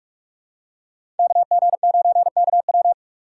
Morse code, heard as a keyed tone, about 46 wpm, 700 Hz. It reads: KG9KW
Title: USA Call Signs - 6x Character Spacing - 45wpm